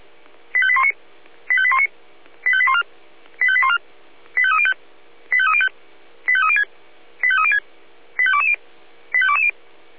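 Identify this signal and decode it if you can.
Five-tone selective call (ZVEI), heard as quick runs of five tones, tones of about 70 ms. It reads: 85819 85819 85814 85814 85305 85305 85306 85306 85220 85220
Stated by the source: ZVEI five-tone selective call received by SDR.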